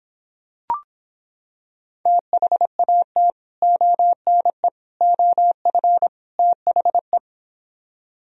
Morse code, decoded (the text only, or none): THAT ONE OF THE